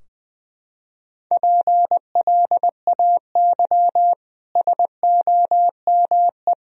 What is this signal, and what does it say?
Morse code, heard as a keyed tone, about 20 wpm, 700 Hz. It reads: PLAY SOME
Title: Sentences from Top 1000 Words - Review Part 3 - 20wpm